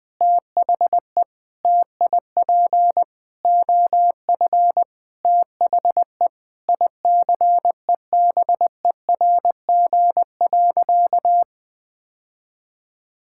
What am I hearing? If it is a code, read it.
Morse code, heard as a keyed tone, about 20 wpm, 700 Hz. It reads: THE TIP OF THE ICEBERG.
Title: Sentences with 7-Word Phrases - 20wpm